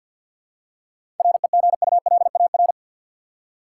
Morse code, heard as a keyed tone, about 53 wpm, 700 Hz. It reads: REGULAR
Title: Seven Letter Words - 50wpm